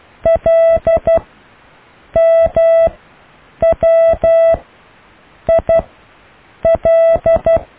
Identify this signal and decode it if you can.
Morse code, heard as a keyed tone, about 12 wpm, 660 Hz. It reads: L M W I L
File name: cw / IJLMU